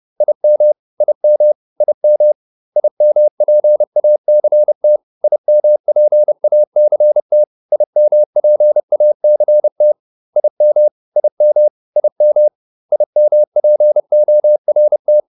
Morse code, heard as a keyed tone, about 30 wpm, 600 Hz. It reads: IM IM IM IMPACT IMPACT IMPACT IM IM IM IMPORT
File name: IM6 30